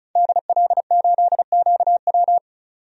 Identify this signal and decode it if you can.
Morse code, heard as a keyed tone, about 34 wpm, 700 Hz. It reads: DL8QW
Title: DXCC-100 Call Signs - 5 Characters - 35wpm